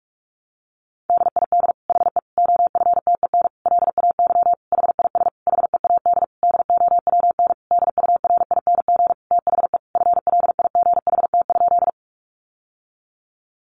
Morse code, heard as a keyed tone, about 44 wpm, 700 Hz. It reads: DID HE OFTEN LAY HIS HEAD DOWN DURING THE FLIGHT?
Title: Sentences from Top 1000 Words 45wpm